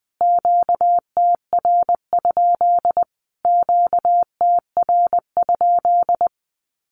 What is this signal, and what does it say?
Morse code, heard as a keyed tone, about 20 wpm, 700 Hz. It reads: QTR? QTR?
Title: Q-Codes Repeated 3x 20wpm